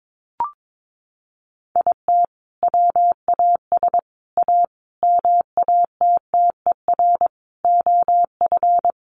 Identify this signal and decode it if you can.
Morse code, heard as a keyed tone, about 22 wpm, 700 Hz. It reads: IT WAS A MATTER OF
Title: Sentences from Top 1500 Words 22wpm